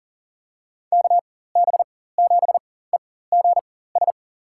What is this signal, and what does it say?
Morse code, heard as a keyed tone, about 40 wpm, spaced as for 19 wpm, 700 Hz. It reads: KB7EGS
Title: USA Call Signs - 6 Characters - 4x Character Spacing - 40wpm